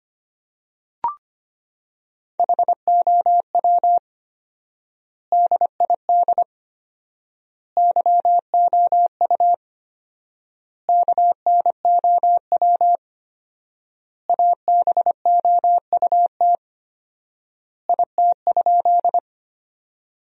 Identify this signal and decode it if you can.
Morse code, heard as a keyed tone, about 25 wpm, 700 Hz. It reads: HOW DID YOU KNOW ABOUT IT?
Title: 4x Word-Spacing - Sentences from Top 100 Words - 25wpm